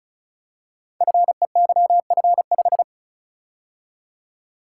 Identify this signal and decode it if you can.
Morse code, heard as a keyed tone, about 36 wpm, 700 Hz. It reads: FEYF5
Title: Code Groups - 5 Letter:Numbers - 35wpm